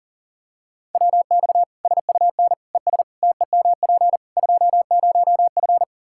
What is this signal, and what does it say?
Morse code, heard as a keyed tone, about 40 wpm, 700 Hz. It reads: WX SUN ES TEMP 20F
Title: Practice QSO - Exchange 2 - Info - 40wpm